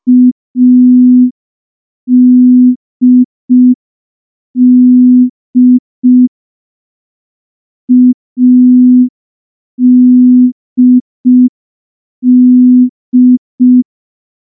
Morse code, heard as a keyed tone, about 5 wpm, 250 Hz. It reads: ADD ADD
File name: morse / morse-training-05WPM